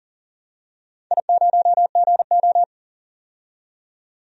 Morse code, heard as a keyed tone, about 40 wpm, 700 Hz. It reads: I0GO
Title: DXCC-100 Call Signs - 4 Characters - 40wpm